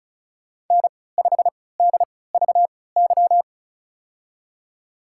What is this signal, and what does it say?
Morse code, heard as a keyed tone, about 34 wpm, spaced as for 20 wpm, 690 Hz. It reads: N5DVY
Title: USA Call Signs - 5 Characters - 3x Character Spacing - 35wpm